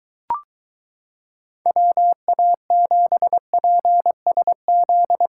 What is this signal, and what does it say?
Morse code, heard as a keyed tone, about 22 wpm, 700 Hz. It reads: WA7PSZ